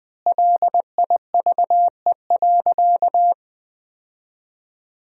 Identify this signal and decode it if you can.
Morse code, heard as a keyed tone, about 21 wpm, 700 Hz. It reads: LIVE.
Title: 2x Word-Spacing - Sentences from Top 300 Words - 20wpm